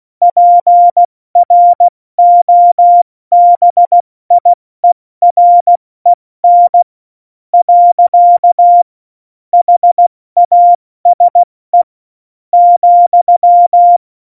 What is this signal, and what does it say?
Morse code, heard as a keyed tone, about 16 wpm, spaced as for 13 wpm, 700 Hz. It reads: PROBIEREN. HASE,
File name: Morsekurs / Witze / Witz 08. Dezember 2020, 70 F80